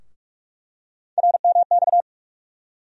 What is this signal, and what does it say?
Morse code, heard as a keyed tone, about 43 wpm, 700 Hz. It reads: RMX
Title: USA Call Sign Postfixes - 3 Letters 45wpm